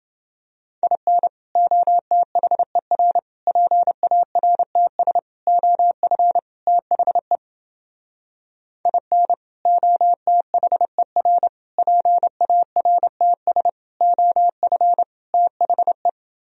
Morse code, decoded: IN OTHER PARTS OF THE IN OTHER PARTS OF THE

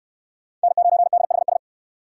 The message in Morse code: N9DHS

